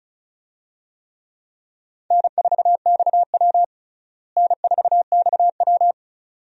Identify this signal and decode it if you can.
Morse code, heard as a keyed tone, about 34 wpm, 700 Hz. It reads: N4XW N4XW
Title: USA Call Signs - 4 Characters - Repeated 2x - 35wpm